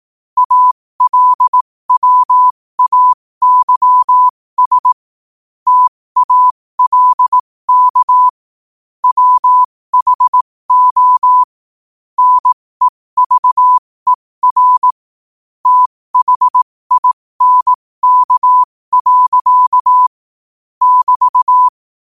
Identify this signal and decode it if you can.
Morse code, heard as a keyed tone, about 18 wpm, spaced as for 15 wpm, 1000 Hz. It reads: ALWAYS TALK WHO NEVER THINK. <BT>
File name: qotdfiles / 15wpm